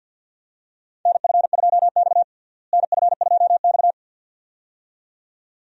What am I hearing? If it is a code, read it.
Morse code, heard as a keyed tone, about 52 wpm, 700 Hz. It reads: NF2X NF2X
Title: USA Call Signs - 4 Characters - Repeated 2x - 50wpm